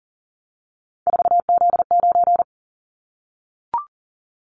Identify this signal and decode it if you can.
Morse code, heard as a keyed tone, about 40 wpm, 700 Hz. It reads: 4Z9
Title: Callsign Prefixes for Top 50 DXCC - 40wpm